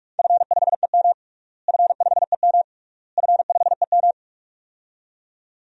Morse code, heard as a keyed tone, about 46 wpm, 700 Hz. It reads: F5EM F5EM F5EM
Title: DXCC-100 Call Signs - 4 Characters - Repeated 3x - 45wpm